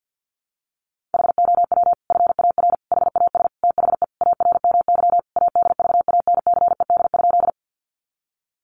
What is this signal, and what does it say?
Morse code, heard as a keyed tone, about 50 wpm, 700 Hz. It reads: HOW FAR HAS THE ARMY ADVANCED?
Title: Sentences from Top 1200 Words 50wpm